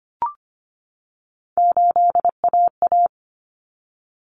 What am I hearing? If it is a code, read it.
Morse code, heard as a keyed tone, about 25 wpm, 700 Hz. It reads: 8AA
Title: Common Cut Numbers - 3-Digit 25wpm